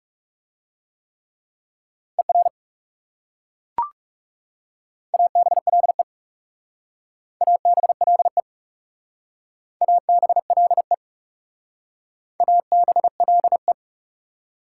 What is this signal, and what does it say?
Morse code, heard as a keyed tone, about 37 wpm, 700 Hz. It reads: ER ABLE ABLE ABLE ABLE
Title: Speed-Racing Prefixes and Suffixes 30wpm